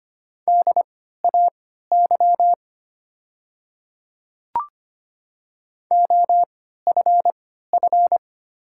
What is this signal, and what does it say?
Morse code, heard as a keyed tone, about 25 wpm, spaced as for 14 wpm, 700 Hz. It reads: DAY OFF
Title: 3x Character Spacing - Top 200 Words - 25wpm